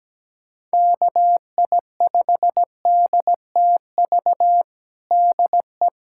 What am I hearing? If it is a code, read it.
Morse code, heard as a keyed tone, about 17 wpm, 700 Hz. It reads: KI5DTV DE
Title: USA Callsign DE Callsign - 6 Characters - 17wpm